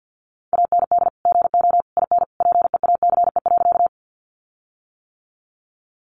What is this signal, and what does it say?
Morse code, heard as a keyed tone, about 50 wpm, 700 Hz. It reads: AND GO IN PEACE.
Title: Sentences from Top 1500 Words 50wpm